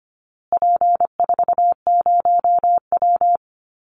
Morse code, heard as a keyed tone, about 25 wpm, 700 Hz. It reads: P40W
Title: Call Signs Repeated 2x - 25wpm